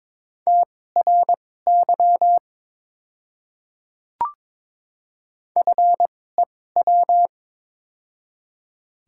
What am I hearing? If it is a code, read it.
Morse code, heard as a keyed tone, about 22 wpm, spaced as for 16 wpm, 690 Hz. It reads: TRY FEW
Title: 2x Character Spacing - Top 500 Words - 22wpm